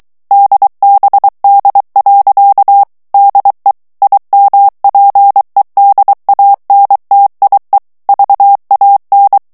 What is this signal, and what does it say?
Morse code, heard as a keyed tone, about 23 wpm, 800 Hz. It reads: DBD. DE IMPEDANTIE VAN
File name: HOLLANDAIS texte clair 1 vitesse 22 mots minute